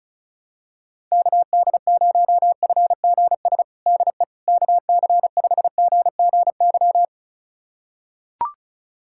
Morse code, heard as a keyed tone, about 36 wpm, 700 Hz. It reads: KD0FGS DE KC5GGY
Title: USA Callsign DE Callsign - 6 Characters - 35wpm